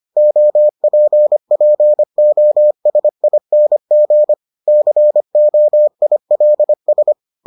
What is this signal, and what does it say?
Morse code, heard as a keyed tone, about 25 wpm, 600 Hz. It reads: OPPOSING COILS